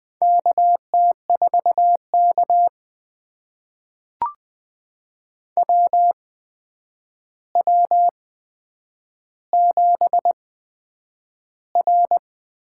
Morse code, heard as a keyed tone, about 20 wpm, spaced as for 9 wpm, 700 Hz. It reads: KT4K WW7R
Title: USA Call Signs - 4 Characters - 8x Character Spacing - 20wpm